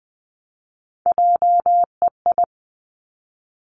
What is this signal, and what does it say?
Morse code, heard as a keyed tone, about 20 wpm, 700 Hz. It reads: JEI
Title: Three Challenging Letter-Number Combinations 20wpm